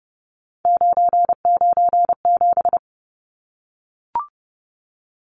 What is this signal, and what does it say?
Morse code, heard as a keyed tone, about 30 wpm, 700 Hz. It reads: 997
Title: Numbers - 3-Digit 30wpm